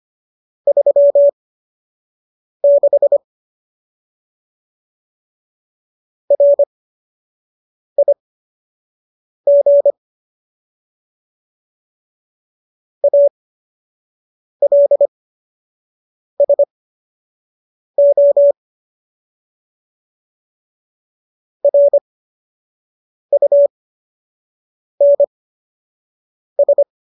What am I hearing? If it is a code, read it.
Morse code, heard as a keyed tone, about 25 wpm, spaced as for 6 wpm, 570 Hz. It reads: 36 RIG ALSO RUNS